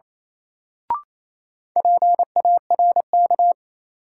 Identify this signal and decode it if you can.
Morse code, heard as a keyed tone, about 28 wpm, 700 Hz. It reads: PARK
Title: Sentences from Top 1300 Words - New Word Drill - 28wpm